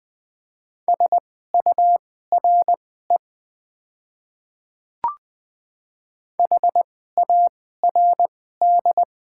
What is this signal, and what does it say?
Morse code, heard as a keyed tone, about 20 wpm, spaced as for 15 wpm, 700 Hz. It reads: SURE HARD
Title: 2x Character Spacing - Top 300 Words - 20wpm